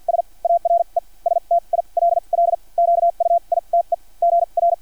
Morse code, heard as a keyed tone, about 50 wpm, spaced as for 38 wpm, 680 Hz. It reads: S ARE STILL QUITE GR
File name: archived / CRSnet